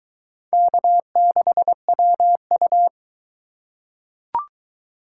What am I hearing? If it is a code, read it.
Morse code, heard as a keyed tone, about 20 wpm, 700 Hz. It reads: K6WU